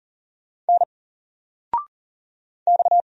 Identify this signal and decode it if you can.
Morse code, heard as a keyed tone, about 39 wpm, 700 Hz. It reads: N X
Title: Single Character (Letter:Number:Punctuation:Prosign) 40wpm